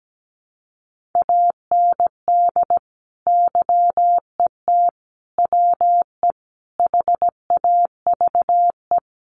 Morse code, heard as a keyed tone, about 17 wpm, 700 Hz. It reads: AND YET WE HAVE
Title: Sentences from Top 1500 Words 17wpm